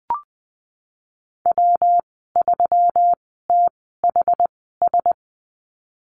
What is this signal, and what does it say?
Morse code, heard as a keyed tone, about 20 wpm, spaced as for 14 wpm, 700 Hz. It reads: W3THS
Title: 179 Lesson 29g - Slash - Call Signs 20wpm